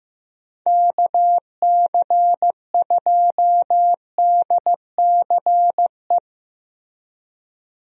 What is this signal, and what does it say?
Morse code, heard as a keyed tone, about 15 wpm, 700 Hz. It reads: KC2DCE
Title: USA Call Signs - 6 Characters - Repeated 3x - 15wpm